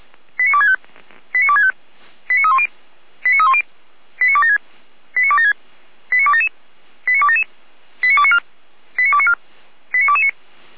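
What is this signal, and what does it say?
Five-tone selective call (ZVEI), heard as quick runs of five tones, tones of about 70 ms. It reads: 89265 89265 89310 89310 89276 89276 89277 89277 89294 89294 89229
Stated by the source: ZVEI five-tone call received by SDR, from the Feuerwehr Orlamünde fire brigade's audio archive.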